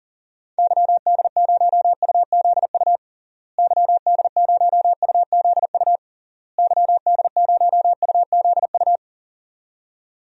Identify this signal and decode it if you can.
Morse code, heard as a keyed tone, about 40 wpm, 700 Hz. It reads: YD0UZU YD0UZU YD0UZU
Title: DXCC-100 Call Signs - 6 Characters - Repeated 3x - 40wpm